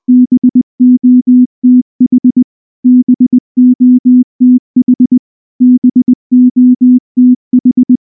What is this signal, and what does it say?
Morse code, heard as a keyed tone, about 19 wpm, 250 Hz. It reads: BOTH BOTH BOTH
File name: morse / morse-training-20WPM-2